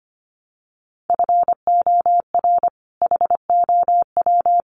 Morse code, heard as a keyed tone, about 25 wpm, 700 Hz. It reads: FOR HOW
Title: Sentences from Top 200 Words 25wpm